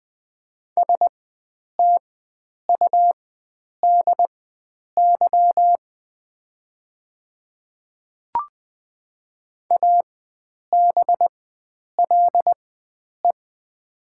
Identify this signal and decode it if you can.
Morse code, heard as a keyed tone, about 20 wpm, spaced as for 9 wpm, 700 Hz. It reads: STUDY ABLE
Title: 4x Character Spacing - Top 500 Words - 20wpm